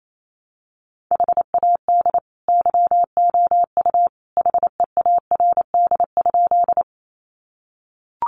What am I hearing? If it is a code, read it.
Morse code, heard as a keyed tone, about 28 wpm, 700 Hz. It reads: HAD YOU HEARD?
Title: Sentences from Top 400 Words 28wpm